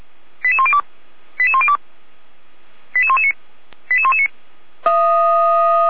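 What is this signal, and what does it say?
Five-tone selective call (ZVEI), heard as quick runs of five tones, tones of about 70 ms. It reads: 88102 88102 88119 88119
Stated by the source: ZVEI five-tone selective call received by SDR.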